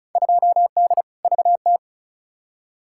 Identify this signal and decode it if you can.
Morse code, heard as a keyed tone, about 36 wpm, 700 Hz. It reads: 2D VT
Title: Field Day Exchanges Encore - 35wpm